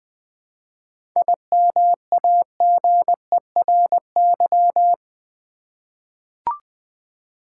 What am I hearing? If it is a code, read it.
Morse code, heard as a keyed tone, about 20 wpm, 700 Hz. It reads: IMAGERY
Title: 7 Letter Words - First 3 Letters Given - 20wpm